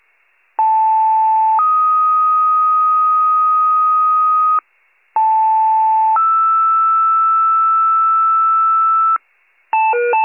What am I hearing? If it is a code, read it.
Two-tone page, heard as a one-second tone, then a three-second tone, about 880 then 1250 Hz, about 880 then 1320 Hz. It is heard from 0.6 to 9.2 s.